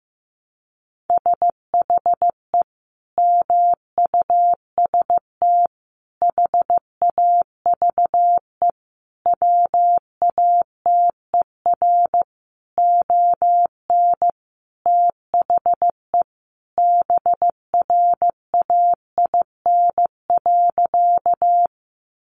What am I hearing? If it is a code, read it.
Morse code, heard as a keyed tone, about 15 wpm, 700 Hz. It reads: SHE MUST HAVE WATER ON THE BRAIN.
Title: Sentences from Top 1100 Words 15wpm